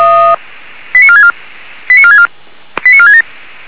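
Five-tone selective call (ZVEI), heard as quick runs of five tones, tones of about 70 ms. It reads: 89464 89464 89467
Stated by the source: ZVEI five-tone call received on SDR.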